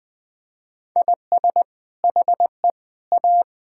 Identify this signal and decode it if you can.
Morse code, heard as a keyed tone, about 20 wpm, 700 Hz. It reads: IS HE A
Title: Sentences from Top 1000 Words - Review Part 6 - 20wpm